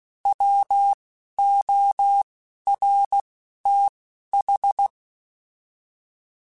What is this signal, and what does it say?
Morse code, heard as a keyed tone, about 16 wpm, 800 Hz. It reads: W O R T H
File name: Alphabetical / Alphabetical mp3s / Ws @10